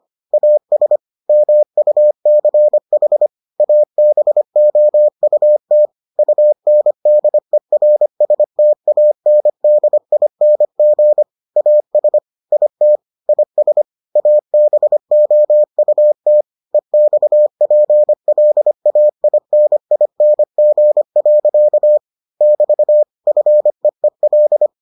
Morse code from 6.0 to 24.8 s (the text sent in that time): UNDERSTANDING AS IT IS ABOUT EXPLAINING. <BT> FEEL